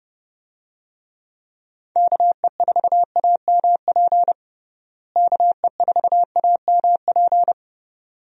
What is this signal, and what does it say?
Morse code, heard as a keyed tone, about 30 wpm, 700 Hz. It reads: KE4AMP KE4AMP
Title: USA Call Signs - 6 Characters - Repeated 2x - 30wpm